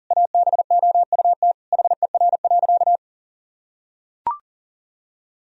Morse code, heard as a keyed tone, about 40 wpm, 700 Hz. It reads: ABOUT HER.
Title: Sentences from Top 1000 Words - Review Part 4 - 40wpm